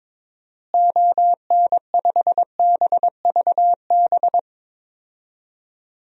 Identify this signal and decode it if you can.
Morse code, heard as a keyed tone, about 22 wpm, 700 Hz. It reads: ON5BVB
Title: DXCC-100 Call Signs - 6 Characters - Repeated 3x - 22wpm